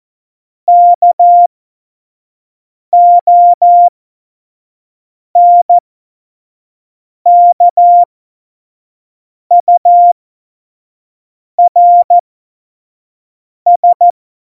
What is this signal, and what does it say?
Morse code, heard as a keyed tone, about 14 wpm, spaced as for 5 wpm, 700 Hz. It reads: KONKURS
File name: Morsekurs / Witze / Witz 04 August 2020, 25 F70